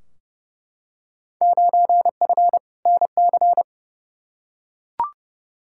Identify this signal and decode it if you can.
Morse code, heard as a keyed tone, about 30 wpm, 700 Hz. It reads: 9F NC
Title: Field Day Exchanges Encore - 30wpm